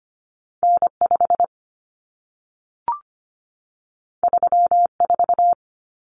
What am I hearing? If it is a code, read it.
Morse code, heard as a keyed tone, about 25 wpm, 700 Hz. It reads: N5 34
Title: Common Cut Numbers - 2-Digit 25wpm